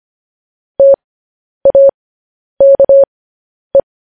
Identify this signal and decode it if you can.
Morse code, heard as a keyed tone, about 25 wpm, spaced as for 10 wpm, 550 Hz. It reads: TAKE
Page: news-headlines-in-morse-code-at-10-wpm